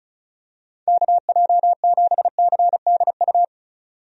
Morse code, heard as a keyed tone, about 36 wpm, 700 Hz. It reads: KJ7CDU
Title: USA Call Signs - 6 Characters - 35wpm